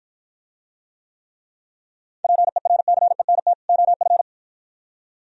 Morse code, heard as a keyed tone, about 53 wpm, 700 Hz. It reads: PERCENT OF